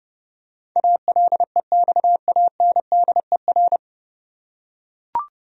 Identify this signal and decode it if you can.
Morse code, heard as a keyed tone, about 31 wpm, 700 Hz. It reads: ALEXANDER
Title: USA Common Names - 30wpm